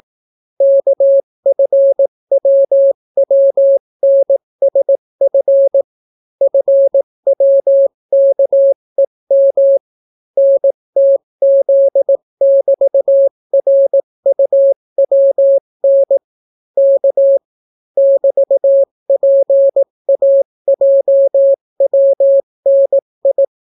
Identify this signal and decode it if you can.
Morse code, heard as a keyed tone, about 18 wpm, 550 Hz. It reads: KFWWNSF FWKEM NTZ<BT>RUWN K <BT>PAJWNI